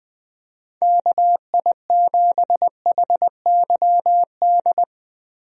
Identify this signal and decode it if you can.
Morse code, heard as a keyed tone, about 21 wpm, 700 Hz. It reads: KI7HYD
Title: USA Call Signs - 6 Characters - Repeated 3x - 20wpm